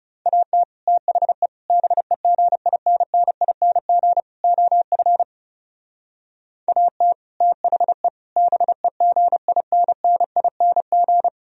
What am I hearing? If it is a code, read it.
Morse code, heard as a keyed tone, about 36 wpm, 700 Hz. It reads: AT THE BEGINNING OF AT THE BEGINNING
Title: Speed-Racing - Sets of 4 Words 30wpm